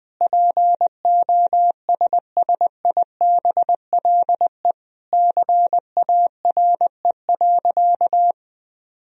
Morse code, heard as a keyed tone, about 21 wpm, 700 Hz. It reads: POSSIBLE CARE.
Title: Sentences from Top 1000 Words - Review Part 2 - 20wpm